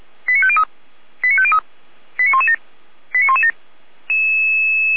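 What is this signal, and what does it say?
Five-tone selective call (ZVEI), heard as quick runs of five tones, tones of about 70 ms. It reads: 89503 89503 89117 89117
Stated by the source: ZVEI five-tone selective call received on an SDR receiver.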